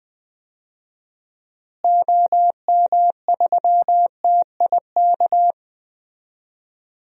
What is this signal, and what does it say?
Morse code, heard as a keyed tone, about 20 wpm, 700 Hz. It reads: OM3TIK
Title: DXCC-100 Call Signs - 6 Characters - Repeated 3x - 20wpm